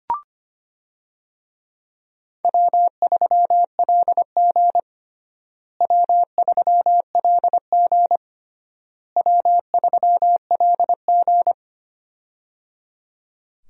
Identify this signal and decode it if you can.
Morse code, heard as a keyed tone, about 25 wpm, 700 Hz. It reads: W3LG W3LG W3LG
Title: USA Call Signs - 4 Characters - Repeated 3x - 25wpm